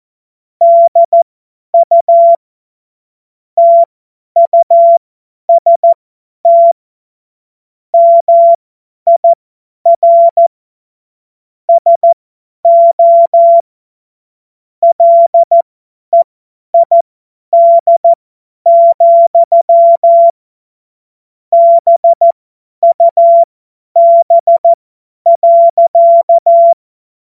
Morse code, heard as a keyed tone, about 14 wpm, spaced as for 10 wpm, 680 Hz. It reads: DU TUST MIR SO LEID, BUB.
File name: Morsekurs / Witze / Witz 19. Oktober 2021, 50 F70